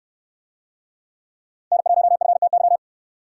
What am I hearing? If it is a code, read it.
Morse code, heard as a keyed tone, about 69 wpm, 700 Hz. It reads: N1FTY